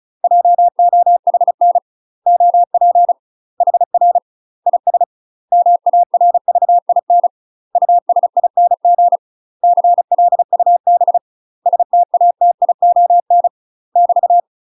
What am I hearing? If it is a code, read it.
Morse code, heard as a keyed tone, about 36 wpm, 700 Hz. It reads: JOHN OP HR IS MARVIN USING CLUB STATION <BT>
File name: UK MASTER QSOs TRAINING / 35WPM / UK MASTER QSOs 0036